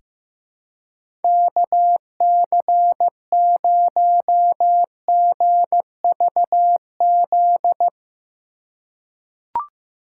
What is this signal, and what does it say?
Morse code, heard as a keyed tone, about 15 wpm, 700 Hz. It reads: KC0GVZ
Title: USA Call Signs - 6 Characters - 2x Character Spacing - 15wpm